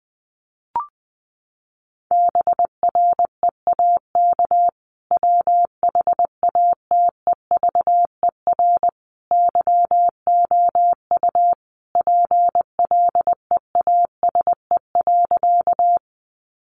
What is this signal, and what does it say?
Morse code, heard as a keyed tone, about 20 wpm, 700 Hz. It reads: BREAK WHATEVER YOU PLEASE.
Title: Sentences from Top 1500 Words 20wpm